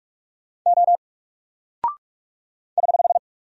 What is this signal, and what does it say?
Morse code, heard as a keyed tone, about 44 wpm, 700 Hz. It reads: O <HH>
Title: Single Character (Letter:Number:Punctuation:Prosign) Alt 45wpm